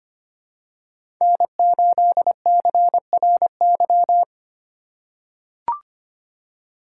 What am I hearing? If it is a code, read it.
Morse code, heard as a keyed tone, about 25 wpm, 700 Hz. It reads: N8CRY
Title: USA Call Signs - 5 Characters - 3x Character Spacing - 25wpm